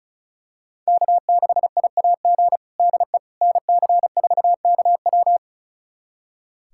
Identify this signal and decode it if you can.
Morse code, heard as a keyed tone, about 36 wpm, 700 Hz. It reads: K6IAG DE NC4KW
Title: USA Callsign DE Callsign - 5 Characters - 35wpm